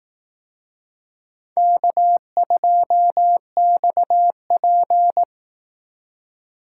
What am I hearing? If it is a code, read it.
Morse code, heard as a keyed tone, about 18 wpm, 700 Hz. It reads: K2XP
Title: Speed Racing - USA Call Signs 15wpm